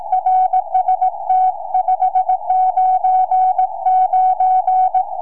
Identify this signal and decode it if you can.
Morse code, heard as a keyed tone, about 18 wpm, 750 Hz. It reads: RST 599
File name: sm4bdq 2m